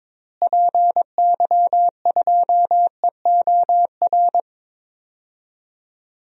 Morse code, heard as a keyed tone, about 22 wpm, 700 Hz. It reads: PY2EOR
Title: DXCC-100 Call Signs - 6 Characters - Repeated 2x - 22wpm